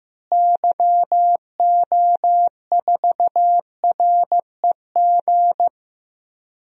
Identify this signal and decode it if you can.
Morse code, heard as a keyed tone, about 15 wpm, 700 Hz. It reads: YO4REG
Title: DXCC-100 Call Signs - 6 Characters - Repeated 3x - 15wpm